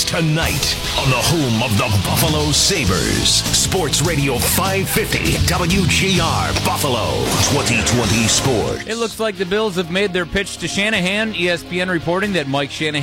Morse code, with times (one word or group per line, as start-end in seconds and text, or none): none